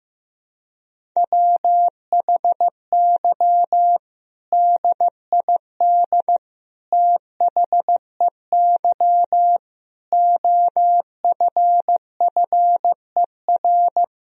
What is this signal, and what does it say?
Morse code, heard as a keyed tone, about 15 wpm, 700 Hz. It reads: WHY DID THEY OFFER
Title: Sentences from Top 1000 Words - Review Part 4 - 15wpm